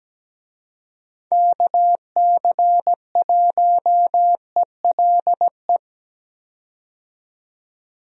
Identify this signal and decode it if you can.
Morse code, heard as a keyed tone, about 17 wpm, 700 Hz. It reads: KC1ELE
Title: USA Call Signs - 6 Characters - 17wpm